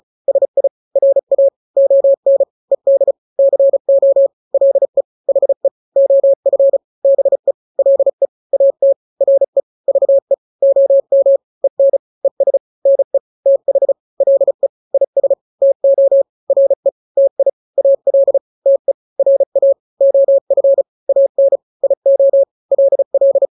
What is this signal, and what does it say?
Morse code, heard as a keyed tone, about 34 wpm, 550 Hz. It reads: SI RA ON ED CO LE HE OF BE LE AT RE VE OM EN ES NE TH LE IS TO RE TI AL TE RA OF AN IO LL
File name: n-grams / 2-grams 35 30 WPM